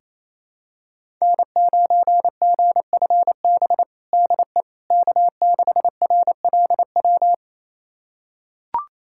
Morse code, heard as a keyed tone, about 28 wpm, 700 Hz. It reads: N9GFB DE K6RLW
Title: USA Callsign DE Callsign - 5 Characters - 28wpm